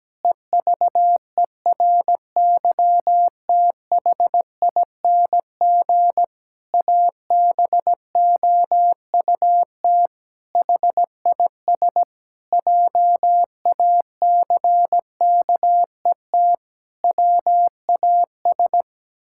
Morse code, heard as a keyed tone, about 17 wpm, 700 Hz. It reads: EVERYTHING ABOUT HIS JACKET WAS
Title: Pangram Challenge v2 - 17wpm